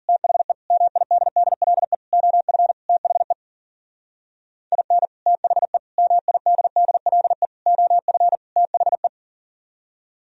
Morse code, heard as a keyed tone, about 50 wpm, 700 Hz. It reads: THE MIDDLE OF THE IN THE MIDDLE OF THE
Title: Speed-Racing - Sets of 5 Words 40wpm